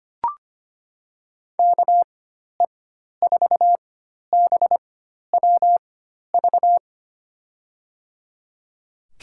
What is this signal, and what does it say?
Morse code, heard as a keyed tone, about 25 wpm, spaced as for 12 wpm, 700 Hz. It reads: KE4BWV
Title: USA Call Signs - 4x Character Spacing - 25wpm